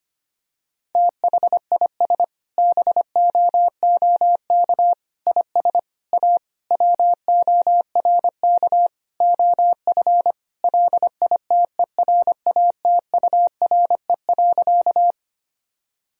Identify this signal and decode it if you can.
Morse code, heard as a keyed tone, about 25 wpm, 700 Hz. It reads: THIS BOOK IS A WORK OF LITERATURE.
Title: Sentences from Top 1000 Words - Review Part 3 - 25wpm